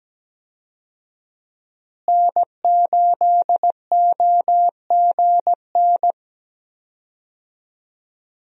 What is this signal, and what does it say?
Morse code, heard as a keyed tone, about 17 wpm, 700 Hz. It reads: N8OGN
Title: USA Call Signs - 5 Characters - Repeated 2x - 17wpm